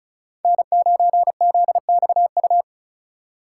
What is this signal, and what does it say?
Morse code, heard as a keyed tone, about 36 wpm, 700 Hz. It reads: N9ZXU
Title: USA Call Signs - 5 Characters - 3x Character Spacing - 35wpm